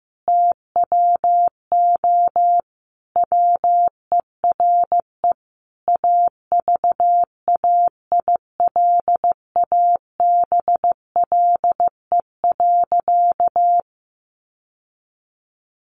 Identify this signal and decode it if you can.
Morse code, heard as a keyed tone, about 15 wpm, 700 Hz. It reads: TWO WERE AVAILABLE.
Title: Sentences from Top 1100 Words 15wpm